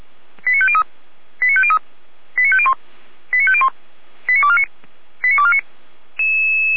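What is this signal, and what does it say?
Five-tone selective call (ZVEI), heard as quick runs of five tones, tones of about 70 ms. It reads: 89503 89503 89501 89501 89249 89249
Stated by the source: ZVEI five-tone selective call received by SDR, recorded by Feuerwehr Orlamünde.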